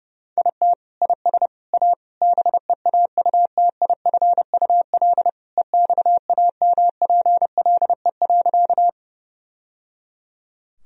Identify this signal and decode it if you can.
Morse code, heard as a keyed tone, about 32 wpm, 700 Hz. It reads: IT IS A BEAUTIFUL EXAMPLE.
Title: Sentences from Top 1000 Words - Review Part 1 - 30wpm